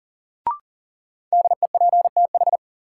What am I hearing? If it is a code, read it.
Morse code, heard as a keyed tone, about 38 wpm, 700 Hz. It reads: DEPTH
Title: Sentences from Top 1200 Words - New Word Drill Alt - 40wpm